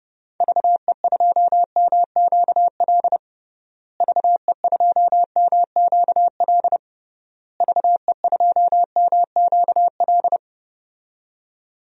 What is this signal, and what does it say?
Morse code, heard as a keyed tone, about 30 wpm, 700 Hz. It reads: VE2MQL VE2MQL VE2MQL
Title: DXCC-100 Call Signs - 6 Characters - Repeated 3x - 30wpm